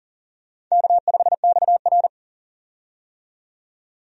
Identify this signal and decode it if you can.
Morse code, heard as a keyed tone, about 40 wpm, 700 Hz. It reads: K5XR
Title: USA Call Signs - 4 Characters - 40wpm